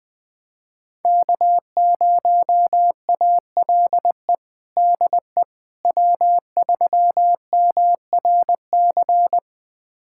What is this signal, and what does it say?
Morse code, heard as a keyed tone, about 20 wpm, 700 Hz. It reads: K0ALE DE W3MRC
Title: USA Callsign DE Callsign - 5 Characters - 20wpm